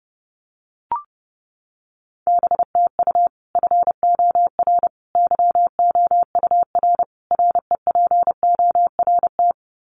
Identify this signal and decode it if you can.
Morse code, heard as a keyed tone, about 30 wpm, 700 Hz. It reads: BTU FOR YOUR REPORT
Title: Common QSO Phrases 30wpm